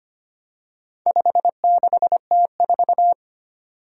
Morse code, heard as a keyed tone, about 25 wpm, 700 Hz. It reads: 56T4
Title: Four Challenging Letter-Number Combinations 25wpm